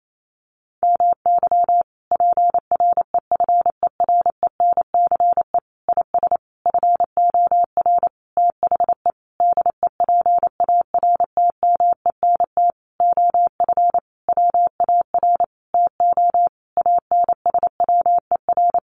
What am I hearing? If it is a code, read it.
Morse code, heard as a keyed tone, about 28 wpm, 700 Hz. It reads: MY PREFERENCE IS FOR THE DEPARTMENT OF WAR TO ANSWER
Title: Sentences from Top 1500 Words 28wpm